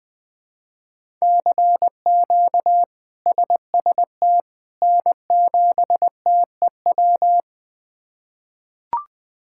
Morse code, heard as a keyed tone, about 21 wpm, 700 Hz. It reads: CQ SST N7TEW
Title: K1USN SST QSO Practice - 20wpm